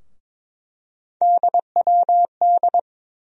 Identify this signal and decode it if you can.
Morse code, heard as a keyed tone, about 22 wpm, 700 Hz. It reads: DWD
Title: USA Call Sign Postfixes - 3 Letters 22wpm